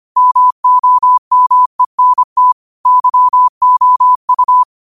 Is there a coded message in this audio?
Yes: Morse code from 0.2 to 4.6 s, heard as a keyed tone, about 25 wpm, 1000 Hz. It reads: MOMENT YOU